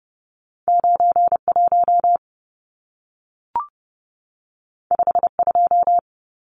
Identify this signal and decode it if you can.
Morse code, heard as a keyed tone, about 30 wpm, 700 Hz. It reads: 91 52
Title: Numbers - 2-Digit 30wpm